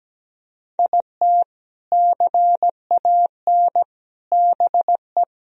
Morse code, heard as a keyed tone, about 17 wpm, 700 Hz. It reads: IT CAN BE
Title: Sentences from Top 1000 Words - Review Part 1 - 17wpm